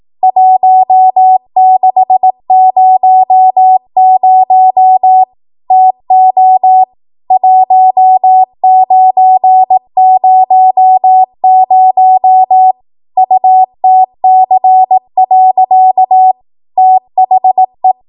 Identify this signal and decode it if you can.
Morse code, heard as a keyed tone, about 18 wpm, 750 Hz. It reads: 1600 TO 1900 UTC. THE
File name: Archive / 18 WPM / 18WPM